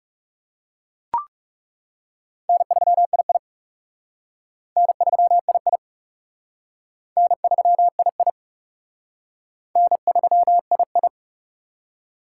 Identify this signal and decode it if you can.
Morse code, heard as a keyed tone, about 37 wpm, 700 Hz. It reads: N3II N3II N3II N3II
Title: Speed Racing - USA 1x2 Call Signs 30wpm